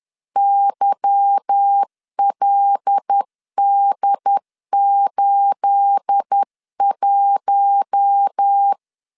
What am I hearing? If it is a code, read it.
Morse code, heard as a keyed tone, about 10 wpm, 800 Hz. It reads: YLD81
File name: LEZIONI / 9 28 10wpm 2842